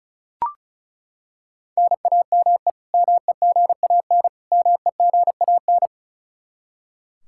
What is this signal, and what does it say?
Morse code, heard as a keyed tone, about 34 wpm, 700 Hz. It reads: NAME MEGAN MEGAN
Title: Practice QSO - Exchange 1 - Beginning - 35wpm